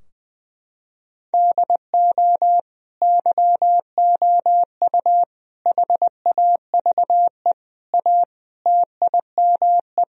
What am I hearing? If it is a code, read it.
Morse code, heard as a keyed tone, about 21 wpm, 700 Hz. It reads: DO YOU HAVE A TIME
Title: Sentences from Top 1000 Words - Review Part 2 - 20wpm